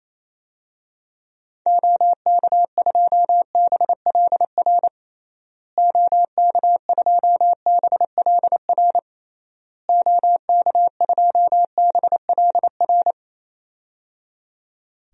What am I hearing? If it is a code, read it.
Morse code, heard as a keyed tone, about 27 wpm, 700 Hz. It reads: OK2BLR OK2BLR OK2BLR
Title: DXCC-100 Call Signs - 6 Characters - Repeated 3x - 28wpm